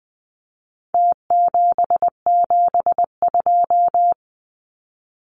Morse code, heard as a keyed tone, about 20 wpm, 700 Hz. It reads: T772
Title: Numbers - Contest Style - 4-Digit 20wpm